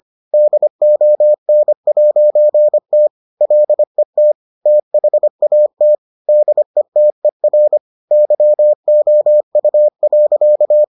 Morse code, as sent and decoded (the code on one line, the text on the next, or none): -.. --- -. .----. - / .-.. - / - .... .- - / -.. - . .-. / -.-- --- ..- .-.-.-
DON'T LET THAT DETER YOU.